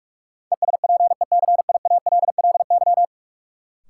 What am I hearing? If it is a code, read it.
Morse code, heard as a keyed tone, about 46 wpm, 700 Hz. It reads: ESPECIALLY
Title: Sentences from Top 700 Words - New Word Drill - 45wpm